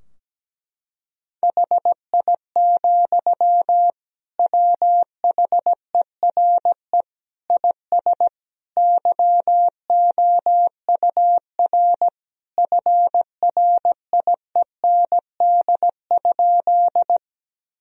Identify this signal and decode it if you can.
Morse code, heard as a keyed tone, about 17 wpm, 700 Hz. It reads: HI, WHERE IS YOUR FRIEND?
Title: Sentences from Top 1000 Words - Review Part 3 - 17wpm